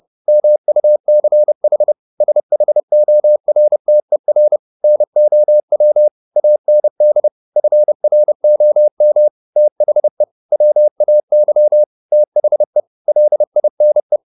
Morse code, heard as a keyed tone, about 30 wpm, 600 Hz. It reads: MUCH SHORTER NOW AND FROM THE WAY THE LINE